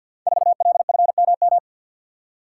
Morse code, heard as a keyed tone, about 52 wpm, 700 Hz. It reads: 4LFMM
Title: Code Groups - 5 Letter:Numbers - 50wpm